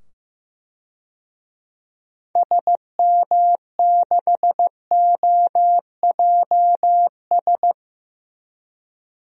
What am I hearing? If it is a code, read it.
Morse code, heard as a keyed tone, about 15 wpm, 700 Hz. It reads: SM6OJS
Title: DXCC-100 Call Signs - 6 Characters - 15wpm